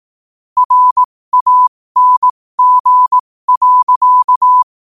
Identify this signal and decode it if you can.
Morse code, heard as a keyed tone, about 18 wpm, spaced as for 15 wpm, 1000 Hz. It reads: RANG.
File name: qotdfiles / 15wpm